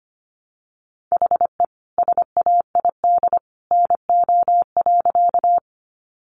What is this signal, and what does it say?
Morse code, heard as a keyed tone, about 25 wpm, 700 Hz. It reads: HE SAID NO.
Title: Sentences from Top 100 Words 25wpm